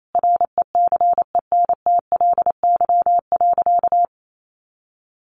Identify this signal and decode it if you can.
Morse code, heard as a keyed tone, about 28 wpm, 700 Hz. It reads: RECENTLY.
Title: Sentences from Top 1200 Words 28wpm